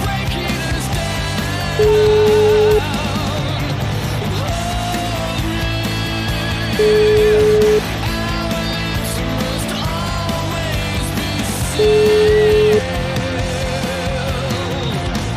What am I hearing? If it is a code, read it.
Ringback tone (Europe, 425 Hz), with noise.